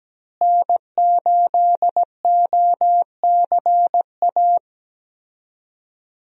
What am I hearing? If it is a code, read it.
Morse code, heard as a keyed tone, about 17 wpm, 700 Hz. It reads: N8OCA
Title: USA Call Signs - 5 Characters - Repeated 3x - 17wpm